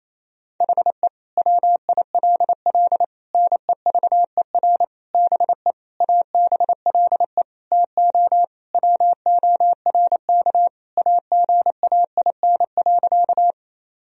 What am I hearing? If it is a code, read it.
Morse code, heard as a keyed tone, about 28 wpm, 700 Hz. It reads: HE WILL NEVER BE ABLE TO WORK AGAIN.
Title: Sentences from Top 1000 Words - Review Part 2 - 28wpm